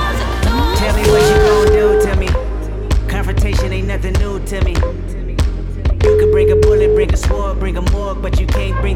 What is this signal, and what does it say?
Ringback tone (Europe, 425 Hz), with noise.